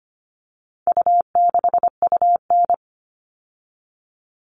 Morse code, heard as a keyed tone, about 25 wpm, 700 Hz. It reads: U6UN
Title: All Cut Numbers - 4-Digit 25wpm